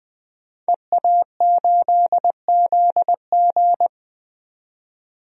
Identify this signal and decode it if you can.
Morse code, heard as a keyed tone, about 20 wpm, 700 Hz. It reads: EA8ZG
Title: DXCC-100 Call Signs - 5 Characters - 20wpm